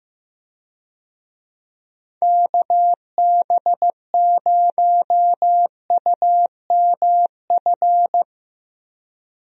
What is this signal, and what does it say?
Morse code, heard as a keyed tone, about 15 wpm, 700 Hz. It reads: KB0UMF
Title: USA Call Signs - 6 Characters - 15wpm